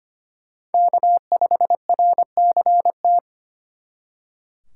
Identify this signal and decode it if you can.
Morse code, heard as a keyed tone, about 33 wpm, 700 Hz. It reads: K5RCT